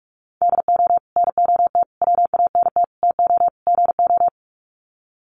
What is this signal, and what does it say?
Morse code, heard as a keyed tone, about 44 wpm, 700 Hz. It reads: DO NOT WANT TO GO
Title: Sets of 5 Words 45wpm